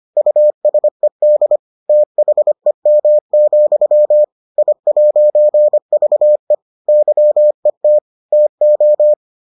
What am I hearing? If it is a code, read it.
Morse code, heard as a keyed tone, about 25 wpm, 600 Hz. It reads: USED THEM, I'VE YET TO